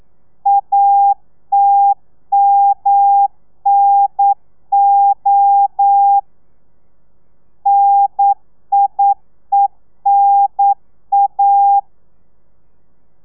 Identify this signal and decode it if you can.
Morse code, heard as a keyed tone, about 9 wpm, 800 Hz. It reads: ATMNO NIENA
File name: telegrafia / i2tfj / Lez2 40